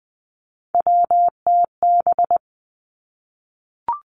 Morse code, heard as a keyed tone, about 20 wpm, 700 Hz. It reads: WTB
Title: Three Challenging Letter-Number Combinations 20wpm